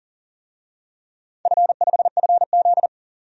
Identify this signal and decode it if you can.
Morse code, heard as a keyed tone, about 40 wpm, 700 Hz. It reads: F5FZ
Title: DXCC-100 Call Signs - 4 Characters - 40wpm